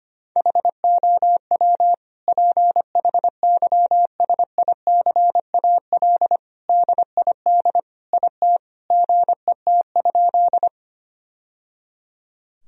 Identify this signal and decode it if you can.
Morse code, heard as a keyed tone, about 25 wpm, 700 Hz. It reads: HOW PHYSICAL DID IT GET?
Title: Sentences from Top 1000 Words - Review Part 4 - 25wpm